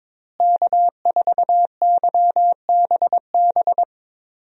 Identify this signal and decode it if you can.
Morse code, heard as a keyed tone, about 22 wpm, 700 Hz. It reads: K4YBB
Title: USA Call Signs - 5 Characters - Repeated 2x - 22wpm